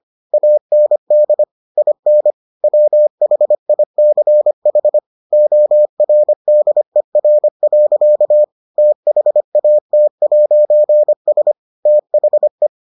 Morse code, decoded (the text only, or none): AND IN WHICH ORDER. THAT'S THE